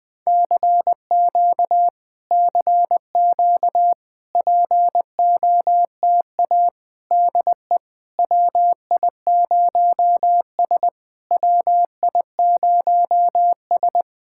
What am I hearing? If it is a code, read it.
Morse code, heard as a keyed tone, about 20 wpm, 690 Hz. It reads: CQ CQ POTA DE WI0S WI0S
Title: POTA QSO Practice - 20wpm